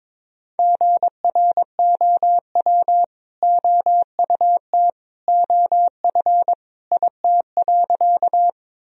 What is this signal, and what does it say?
Morse code, heard as a keyed tone, about 22 wpm, 700 Hz. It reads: GROW OUT OF IT.
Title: Sentences from Top 1000 Words - Review Part 1 - 22wpm